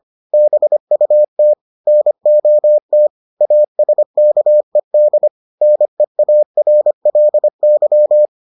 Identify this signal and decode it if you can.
Morse code, heard as a keyed tone, about 25 wpm, 600 Hz. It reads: BUT NOT ASKED NEARLY